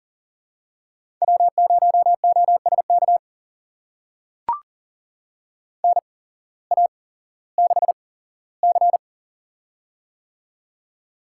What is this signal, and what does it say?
Morse code, heard as a keyed tone, about 40 wpm, spaced as for 19 wpm, 700 Hz. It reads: W0OSK NA6C
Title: USA Call Signs - 8x Character Spacing - 40wpm